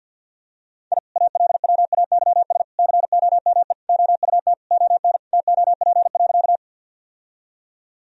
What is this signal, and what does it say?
Morse code, heard as a keyed tone, about 47 wpm, 700 Hz. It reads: I ALWAYS COME OUT ON TOP.
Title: Sentences from Top 1000 Words - Review Part 2 - 50wpm